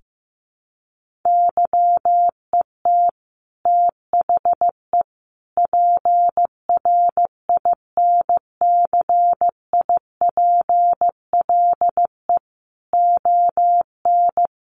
Morse code, heard as a keyed tone, about 15 wpm, 700 Hz. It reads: YET THE PRINCIPLE ON
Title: Sentences from Top 1400 Words 15wpm